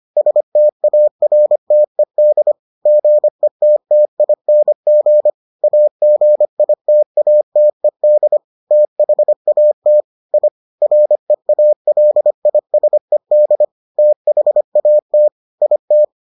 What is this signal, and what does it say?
Morse code, heard as a keyed tone, about 25 wpm, 600 Hz. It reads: STARTED GETTING AGITATED THAT I REALISED THAT IT